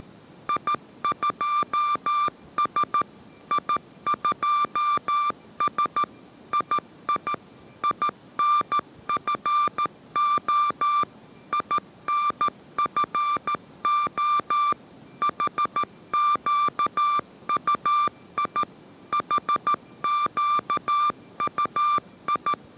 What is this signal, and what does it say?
Morse code, heard as a keyed tone, about 14 wpm, 1180 Hz. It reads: I2S I2S II INFO INFO HQUI HQUI